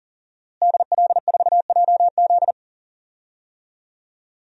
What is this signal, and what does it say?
Morse code, heard as a keyed tone, about 40 wpm, 700 Hz. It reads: DL4JZ
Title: DXCC-100 Call Signs - 5 Characters - Repeated 3x - 40wpm